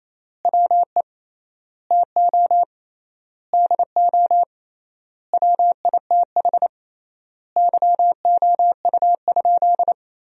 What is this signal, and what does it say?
Morse code, heard as a keyed tone, about 28 wpm, 700 Hz. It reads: WE TO DO WITH YOU?
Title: 3x Word-Spacing - Sentences from Top 100 Words - 28wpm